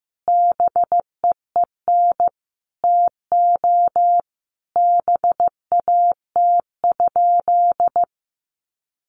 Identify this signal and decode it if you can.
Morse code, heard as a keyed tone, about 15 wpm, 700 Hz. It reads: BEEN TO BAT?
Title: Sentences from Top 1500 Words 15wpm